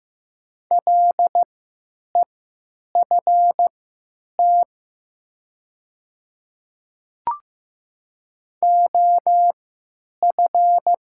Morse code, heard as a keyed tone, about 15 wpm, spaced as for 8 wpm, 700 Hz. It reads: LEFT OF
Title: 3x Character Spacing - Top 300 Words - 15wpm